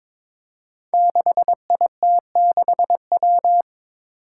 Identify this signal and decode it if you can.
Morse code, heard as a keyed tone, about 21 wpm, 700 Hz. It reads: 6IT6W
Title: Five Challenging Letter-Number Combinations 22wpm